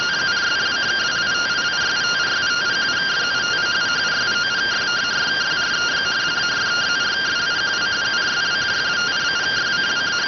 Two-tone teleprinter signal, about 170 Hz shift.